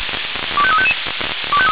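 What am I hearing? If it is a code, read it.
Five-tone selective call (ZVEI), heard as quick runs of five tones, tones of about 70 ms. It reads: 25355
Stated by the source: ZVEI five-tone call 25299 received by SDR.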